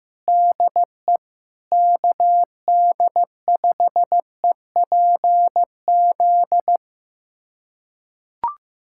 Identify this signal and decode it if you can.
Morse code, heard as a keyed tone, about 16 wpm, 700 Hz. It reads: DE KD5EPZ
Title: USA Callsign DE Callsign - 6 Characters - 15wpm